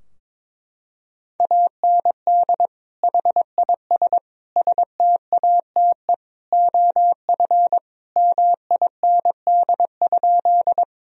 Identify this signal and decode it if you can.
Morse code, heard as a keyed tone, about 22 wpm, 700 Hz. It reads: AND HIS STATE OF MIND?
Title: Sentences from Top 1000 Words - Review Part 2 - 22wpm